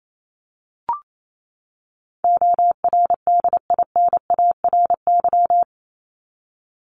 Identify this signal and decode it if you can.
Morse code, heard as a keyed tone, about 28 wpm, 700 Hz. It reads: ORDINARY
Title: Top 1751-2000 Words 28wpm